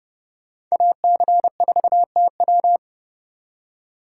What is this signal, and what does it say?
Morse code, heard as a keyed tone, about 39 wpm, 700 Hz. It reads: AC4TW